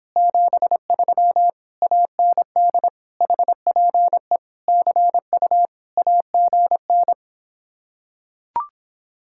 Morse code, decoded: 73 AND HPE CU AGN

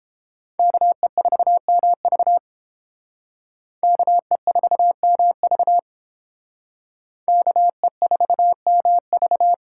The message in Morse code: KE4MV KE4MV KE4MV

